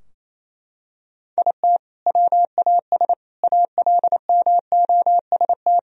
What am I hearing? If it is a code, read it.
Morse code, heard as a keyed tone, about 28 wpm, 700 Hz. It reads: IT WAS ALMOST
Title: Sentences from Top 1000 Words - Review Part 4 - 28wpm